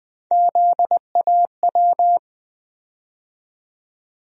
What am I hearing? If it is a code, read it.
Morse code, heard as a keyed tone, about 21 wpm, 700 Hz. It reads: ZAW
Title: USA Call Sign Postfixes - 3 Letters 20wpm